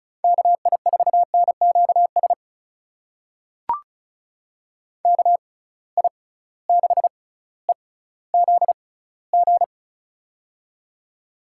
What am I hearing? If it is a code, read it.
Morse code, heard as a keyed tone, about 36 wpm, spaced as for 18 wpm, 700 Hz. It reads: KI4NQS KI6EZG
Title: USA Call Signs - 6 Characters - 6x Character Spacing - 35wpm